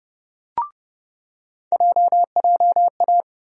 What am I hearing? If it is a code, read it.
Morse code, heard as a keyed tone, about 32 wpm, 700 Hz. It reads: JJA
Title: USA Call Sign Postfixes - 3 Letters 30wpm